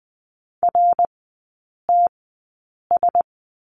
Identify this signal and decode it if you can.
Morse code, heard as a keyed tone, about 20 wpm, spaced as for 8 wpm, 690 Hz. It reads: RTS
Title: 081 Lesson 14e - Number 2 - Three Characters 20wpm